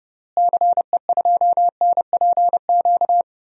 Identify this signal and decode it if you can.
Morse code, heard as a keyed tone, about 29 wpm, 700 Hz. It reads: CE2NPQ
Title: DXCC-100 Call Signs - 6 Characters - Repeated 3x - 30wpm